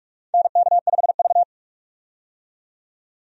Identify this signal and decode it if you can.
Morse code, heard as a keyed tone, about 44 wpm, 700 Hz. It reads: NK5V
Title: USA Call Signs - 4 Characters - 45wpm